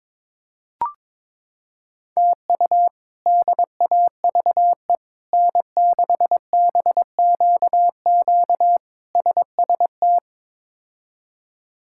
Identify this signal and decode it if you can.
Morse code, heard as a keyed tone, about 22 wpm, 700 Hz. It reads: TU DAVE N6BQQ SST
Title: K1USN SST QSO Practice - 22wpm